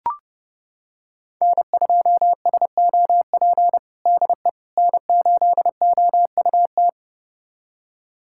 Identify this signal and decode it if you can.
Morse code, heard as a keyed tone, about 32 wpm, 700 Hz. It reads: N2SOP DE N8OUT
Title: USA Callsign DE Callsign - 5 Characters - 30wpm